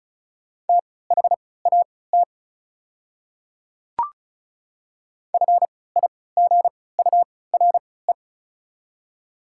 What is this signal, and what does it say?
Morse code, heard as a keyed tone, about 36 wpm, spaced as for 20 wpm, 690 Hz. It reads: THAT FIGURE
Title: 3x Character Spacing - Top 500 Words - 35wpm